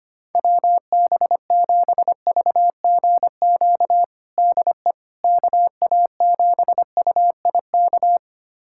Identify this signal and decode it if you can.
Morse code, heard as a keyed tone, about 25 wpm, 700 Hz. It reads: WB7VGQ DE KA7UIK
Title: USA Callsign DE Callsign - 6 Characters - 25wpm